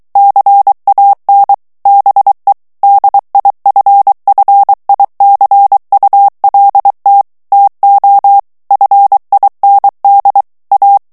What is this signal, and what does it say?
Morse code, heard as a keyed tone, about 23 wpm, 800 Hz. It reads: CAN BE DIFFICULT TO FIND A
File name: morse mp3 / ANGLAIS texte clair 1 vitesse 22 mots minute